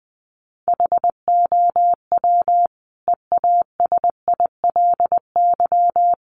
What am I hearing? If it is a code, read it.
Morse code, heard as a keyed tone, about 20 wpm, 700 Hz. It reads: HOW EASILY
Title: Sentences from Top 900 Words 20wpm